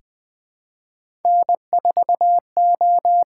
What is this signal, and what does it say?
Morse code, heard as a keyed tone, about 20 wpm, 700 Hz. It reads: N4O
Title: USA Call Signs - 5 Characters - 6x Character Spacing - 20wpm